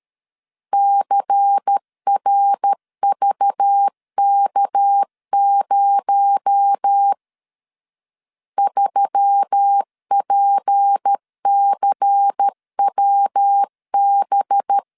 Morse code, heard as a keyed tone, about 13 wpm, 800 Hz. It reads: CRVK0 3PCWB
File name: LEZIONI / 6 PKX0 / 6 PKX0 12wpm 2231